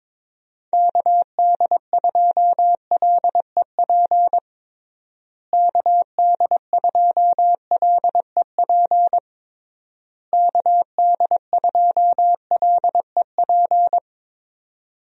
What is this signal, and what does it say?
Morse code, heard as a keyed tone, about 22 wpm, 700 Hz. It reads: KD2LEP KD2LEP KD2LEP
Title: USA Call Signs - 6 Characters - Repeated 3x - 22wpm